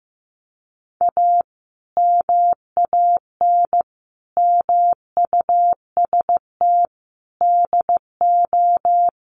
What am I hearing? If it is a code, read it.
Morse code, heard as a keyed tone, about 15 wpm, 700 Hz. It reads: A MAN MUST DO
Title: Sentences from Top 1500 Words 15wpm